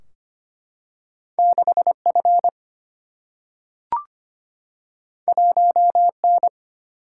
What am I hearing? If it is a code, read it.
Morse code, heard as a keyed tone, about 25 wpm, 700 Hz. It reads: 6 F 1 N
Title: Single-Letter Single-Number 25wpm